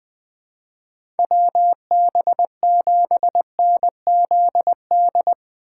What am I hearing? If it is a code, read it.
Morse code, heard as a keyed tone, about 21 wpm, 700 Hz. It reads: WB7NZD
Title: USA Call Signs - 3x Character Spacing - 20wpm